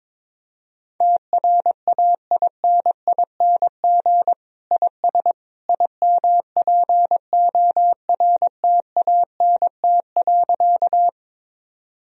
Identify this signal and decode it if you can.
Morse code, heard as a keyed tone, about 21 wpm, 700 Hz. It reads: TRAINING IS IMPORTANT.
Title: Sentences from Top 1000 Words - Review Part 4 - 22wpm